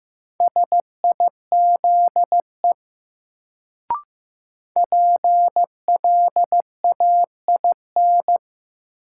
Morse code, heard as a keyed tone, about 15 wpm, 700 Hz. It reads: SIZE PLAIN
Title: Sentences from Top 400 Words - New Word Drill - 15wpm